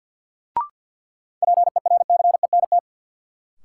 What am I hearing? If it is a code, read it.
Morse code, heard as a keyed tone, about 51 wpm, 700 Hz. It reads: PERCENT
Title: Sets of 2 Words - New Word Drill - 50wpm